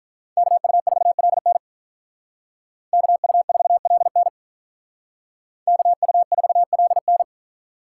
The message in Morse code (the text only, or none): KU4LN KU4LN KU4LN